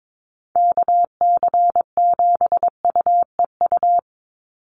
Morse code, heard as a keyed tone, about 22 wpm, 700 Hz. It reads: KC7UEU
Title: USA Call Signs 22wpm